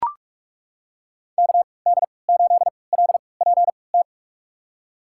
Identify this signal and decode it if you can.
Morse code, heard as a keyed tone, about 41 wpm, spaced as for 25 wpm, 700 Hz. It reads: KD8LPT
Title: USA Call Signs - 6 Characters - 3x Character Spacing - 45wpm